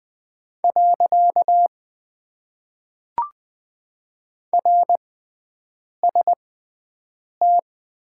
Morse code, heard as a keyed tone, about 20 wpm, spaced as for 7 wpm, 700 Hz. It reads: . RST
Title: Top 90 QSO Elements - 6x Character Spacing - 20wpm